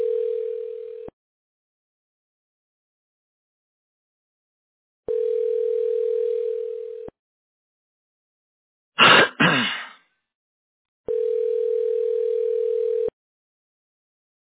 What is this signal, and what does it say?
Ringback tone (North America), 440 + 480 Hz.